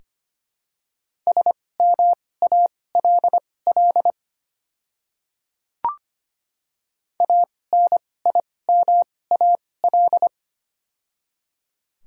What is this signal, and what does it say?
Morse code, heard as a keyed tone, about 25 wpm, spaced as for 18 wpm, 700 Hz. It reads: SMALL ANIMAL
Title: 2x Character Spacing - Top 200 Words - 25wpm